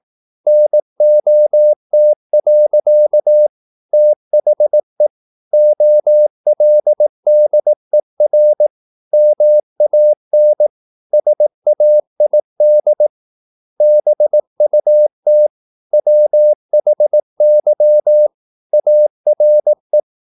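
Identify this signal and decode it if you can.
Morse code, heard as a keyed tone, about 18 wpm, 600 Hz. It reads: NOT. THE OLDER MAN SAID BUT WHY ARE
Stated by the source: Morse practice at 18 wpm, English text, from CW med Gnister.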